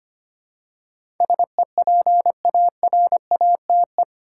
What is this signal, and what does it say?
Morse code, heard as a keyed tone, about 25 wpm, 700 Hz. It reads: SEPARATE
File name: Eight Letter Words - 25wpm